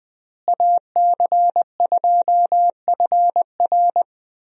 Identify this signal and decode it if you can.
Morse code, heard as a keyed tone, about 20 wpm, 700 Hz. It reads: AC2FR
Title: USA Call Signs - 5 Characters - Repeated 3x - 20wpm